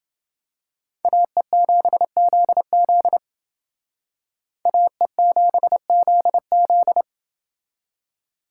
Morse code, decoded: AE7ZZ AE7ZZ

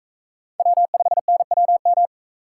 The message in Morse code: W5NWM

